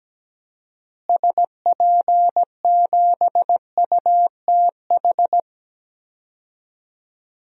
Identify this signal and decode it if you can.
Morse code, heard as a keyed tone, about 17 wpm, 700 Hz. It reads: SP7UTH
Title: DXCC-100 Call Signs - 6 Characters - Repeated 2x - 17wpm